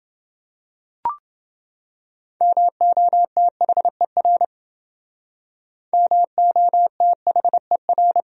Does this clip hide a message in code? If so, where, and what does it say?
Yes: Morse code from 2.4 to 8.2 s, heard as a keyed tone, about 29 wpm, 700 Hz. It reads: MOTHER MOTHER